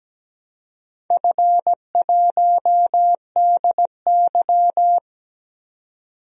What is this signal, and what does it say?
Morse code, heard as a keyed tone, about 17 wpm, 700 Hz. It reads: F1DY
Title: DXCC-100 Call Signs - 4 Characters - 17wpm